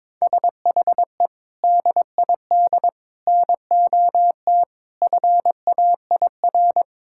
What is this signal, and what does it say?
Morse code, heard as a keyed tone, about 22 wpm, 700 Hz. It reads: SHE DID NOT FAIR
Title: Sentences from Top 1000 Words - Review Part 5 - 22wpm